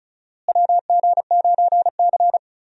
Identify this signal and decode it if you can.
Morse code, heard as a keyed tone, about 36 wpm, 700 Hz. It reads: WG9C
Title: USA Call Signs - 4 Characters - Repeated 3x - 35wpm